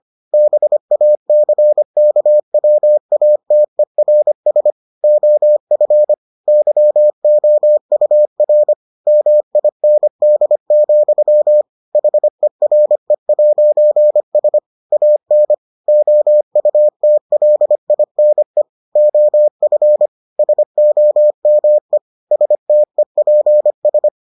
Morse code, heard as a keyed tone, about 25 wpm, 600 Hz. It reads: BACKWATERS OF YOUR MIND, HERE'S AN OUTLINE OF SOME STEPS